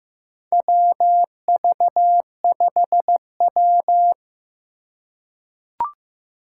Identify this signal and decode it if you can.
Morse code, heard as a keyed tone, about 15 wpm, 700 Hz. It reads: WV5W
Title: USA Call Signs - 4 Characters - 3x Character Spacing - 15wpm